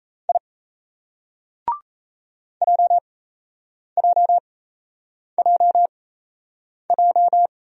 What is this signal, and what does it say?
Morse code, heard as a keyed tone, about 38 wpm, spaced as for 8 wpm, 700 Hz. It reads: I JJJJ